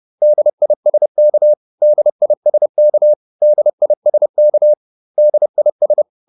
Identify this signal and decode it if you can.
Morse code, heard as a keyed tone, about 30 wpm, 600 Hz. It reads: DISK DISK DISK DIS